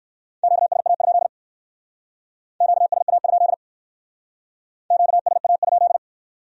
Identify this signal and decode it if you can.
Morse code, heard as a keyed tone, about 58 wpm, 700 Hz. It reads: QSA? QSA? QSA?